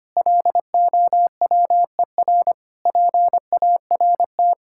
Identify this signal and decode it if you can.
Morse code, heard as a keyed tone, about 25 wpm, 700 Hz. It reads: LOWER PART
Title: Sentences from Top 1000 Words - Review Part 5 - 25wpm